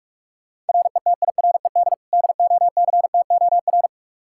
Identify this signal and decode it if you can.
Morse code, heard as a keyed tone, about 46 wpm, 700 Hz. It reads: RETIRED DOCTOR
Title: K1USN SST QSO Practice 45wpm